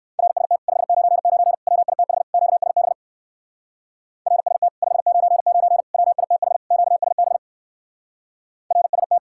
Morse code, heard as a keyed tone, about 58 wpm, 700 Hz. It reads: RST 599 WITH QSB RST 599 WITH QSB RST